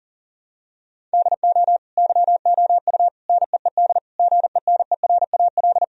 Morse code, heard as a keyed tone, about 40 wpm, 700 Hz. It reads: DO YOU NEED GENERAL
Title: Sentences from Top 1000 Words - Review Part 3 - 40wpm